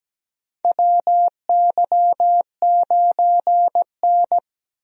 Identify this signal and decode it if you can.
Morse code, heard as a keyed tone, about 17 wpm, 700 Hz. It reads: WY9N
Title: USA Call Signs - 4 Characters - 17wpm